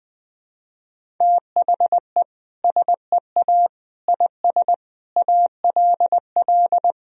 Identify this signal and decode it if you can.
Morse code, heard as a keyed tone, about 19 wpm, 700 Hz. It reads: THE SEA IS ALL
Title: Sentences from Top 1000 Words - Review Part 1 - 20wpm